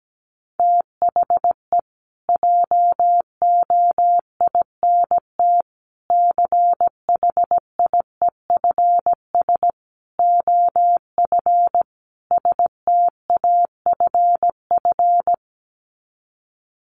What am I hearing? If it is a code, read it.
Morse code, heard as a keyed tone, about 17 wpm, 700 Hz. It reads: THE JOINT CHIEFS OF STAFF
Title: Sets of 5 Words Encore2 17wpm